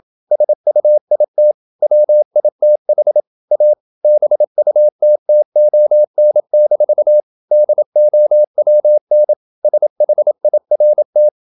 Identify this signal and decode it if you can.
Morse code, heard as a keyed tone, about 27 wpm, 600 Hz. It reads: SUIT WITH A BUTTON-DOWN SHIRT
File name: Never 0027wpm